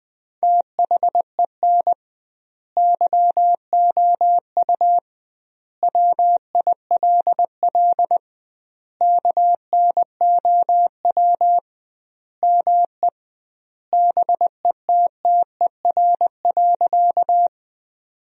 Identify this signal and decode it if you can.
Morse code, heard as a keyed tone, about 20 wpm, 700 Hz. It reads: THEN YOU WILL KNOW ME BETTER.
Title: 2x Word-Spacing - Sentences from Top 400 Words - 20wpm